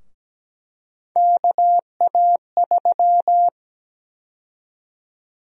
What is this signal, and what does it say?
Morse code, heard as a keyed tone, about 17 wpm, 700 Hz. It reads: KA3
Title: USA Call Sign Prefixes - Repeated 2x - 17wpm